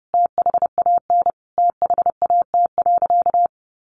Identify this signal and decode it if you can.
Morse code, heard as a keyed tone, about 30 wpm, 700 Hz. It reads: THAN THAT.
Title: Sentences from Top 400 Words 30wpm